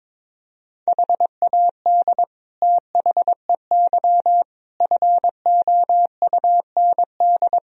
Morse code, heard as a keyed tone, about 22 wpm, 700 Hz. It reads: HAD THEY FOUND
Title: Sentences from Top 1000 Words - Review Part 2 - 22wpm